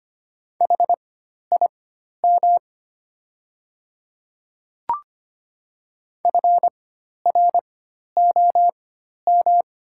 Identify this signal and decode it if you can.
Morse code, heard as a keyed tone, about 25 wpm, spaced as for 12 wpm, 700 Hz. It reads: HIM FROM
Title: 4x Character Spacing - Top 300 Words - 25wpm